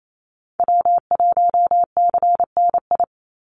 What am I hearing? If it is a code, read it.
Morse code, heard as a keyed tone, about 28 wpm, 700 Hz. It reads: W1CNI
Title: USA Call Signs 28wpm